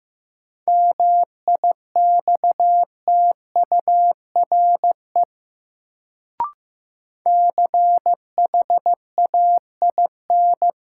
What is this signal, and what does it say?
Morse code, heard as a keyed tone, about 15 wpm, 700 Hz. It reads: MIXTURE CHAIN
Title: Sentences from Top 1500 Words - New Word Drill Alt - 15wpm